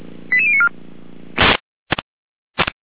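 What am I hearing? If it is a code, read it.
Five-tone selective call (ZVEI), heard as quick runs of five tones, tones of about 70 ms. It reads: 88094
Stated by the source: Five-tone ZVEI selective call received on an SDR.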